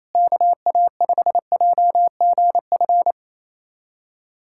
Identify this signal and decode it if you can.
Morse code, heard as a keyed tone, about 28 wpm, 700 Hz. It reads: KA5JGF
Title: USA Callsign DE Callsign - 6 Characters - 28wpm